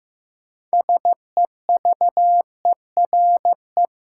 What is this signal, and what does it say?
Morse code, heard as a keyed tone, about 15 wpm, 700 Hz. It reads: SEVERE
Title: Sentences from Top 1400 Words - New Word Drill Alt - 15wpm